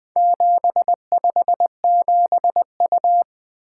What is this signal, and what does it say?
Morse code, heard as a keyed tone, about 20 wpm, 700 Hz. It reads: 757U
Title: Four Letter-Number Combinations - 20wpm